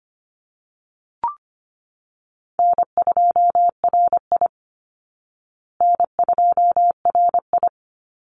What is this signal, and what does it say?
Morse code, heard as a keyed tone, about 25 wpm, 700 Hz. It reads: N2RI N2RI
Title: Easy Call Signs Repeated 2x - 25wpm